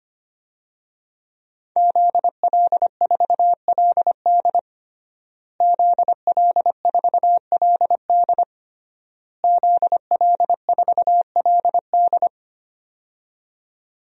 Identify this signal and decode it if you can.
Morse code, heard as a keyed tone, about 25 wpm, 700 Hz. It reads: ZL4LD ZL4LD ZL4LD
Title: DXCC-100 Call Signs - 5 Characters - Repeated 3x - 25wpm